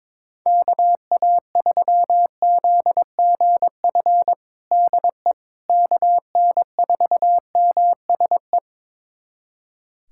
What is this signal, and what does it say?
Morse code, heard as a keyed tone, about 22 wpm, 700 Hz. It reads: KA3ZGF DE KN4MSE
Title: USA Callsign DE Callsign 22wpm